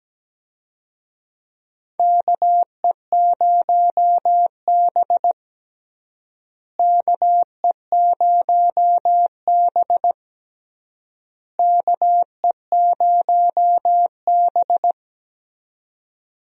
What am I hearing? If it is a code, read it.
Morse code, heard as a keyed tone, about 17 wpm, 700 Hz. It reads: KE0B KE0B KE0B
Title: USA Call Signs - 4 Characters - Repeated 3x - 17wpm